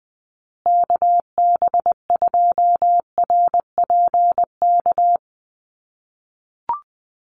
Morse code, heard as a keyed tone, about 20 wpm, 700 Hz. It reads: KB2RPK
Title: USA Call Signs Repeated 2x 20wpm